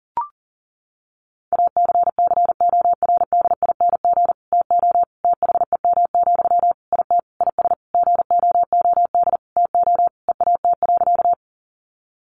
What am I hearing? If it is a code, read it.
Morse code, heard as a keyed tone, about 40 wpm, 700 Hz. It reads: ACCORDING TO THEM, IT IS GOOD TO EAT.
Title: Sentences from Top 800 Words 40wpm